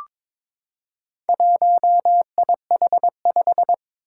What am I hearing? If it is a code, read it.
Morse code, heard as a keyed tone, about 22 wpm, 700 Hz. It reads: 1IH5
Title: Four Challenging Letter-Number Combinations 22wpm